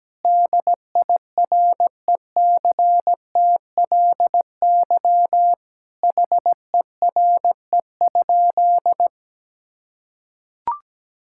Morse code, heard as a keyed tone, about 17 wpm, 700 Hz. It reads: DIRECTLY HERE?
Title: Sentences from Top 1000 Words - Review Part 5 - 17wpm